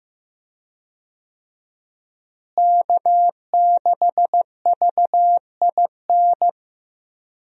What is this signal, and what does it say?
Morse code, heard as a keyed tone, about 15 wpm, 700 Hz. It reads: K6VIN
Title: USA Call Signs - 5 Characters - Repeated 3x - 15wpm